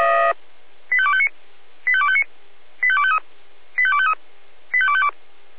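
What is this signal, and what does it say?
Five-tone selective call (ZVEI), heard as quick runs of five tones, tones of about 70 ms. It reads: 85259 85259 85253 85253 85252
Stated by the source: ZVEI five-tone selective call received on an SDR.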